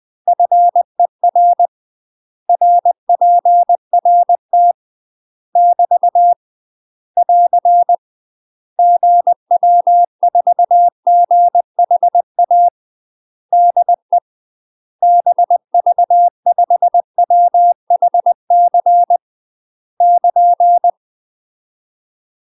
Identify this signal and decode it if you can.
Morse code, heard as a keyed tone, about 20 wpm, 700 Hz. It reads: FER RPRT <BT> <AR> GW4GHA DE BV5WHC <KN>
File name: UK MASTER QSOs TRAINING / 20WPM / UK MASTER QSOs 0062